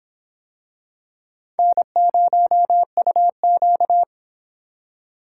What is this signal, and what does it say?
Morse code, heard as a keyed tone, about 24 wpm, 700 Hz. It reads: N0UQ